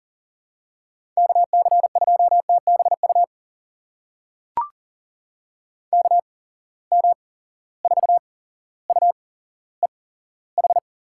Morse code, heard as a keyed tone, about 40 wpm, spaced as for 17 wpm, 700 Hz. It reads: KC2TBU KM4UEH
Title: USA Call Signs - 6 Characters - 8x Character Spacing - 40wpm